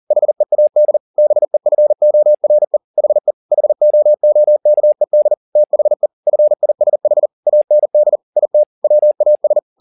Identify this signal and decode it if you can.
Morse code, heard as a keyed tone, about 40 wpm, 600 Hz. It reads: HEAD BEFORE HE HOOKED THE FISH AND IT WAS